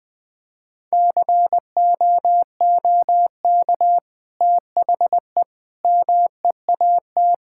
Morse code, heard as a keyed tone, about 20 wpm, 700 Hz. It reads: COOK THE MEAT
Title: Sentences from Top 1000 Words - Review Part 6 - 20wpm